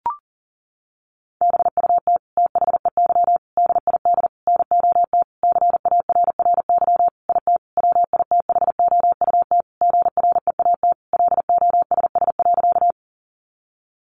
Morse code, heard as a keyed tone, about 40 wpm, 700 Hz. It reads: BUT THEY DID NOT CARRY IT WITHOUT GREAT LOSS.
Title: Sentences from Top 1400 Words 40wpm